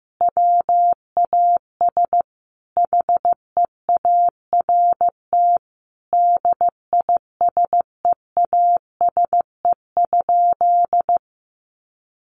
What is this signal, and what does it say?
Morse code, heard as a keyed tone, about 15 wpm, 700 Hz. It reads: WAS HEART DISEASE?
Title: Sentences from Top 1200 Words 15wpm